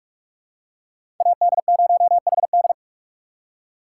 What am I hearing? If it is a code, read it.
Morse code, heard as a keyed tone, about 47 wpm, 690 Hz. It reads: AD0HD
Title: USA Call Signs - 5 Characters - 6x Character Spacing - 45wpm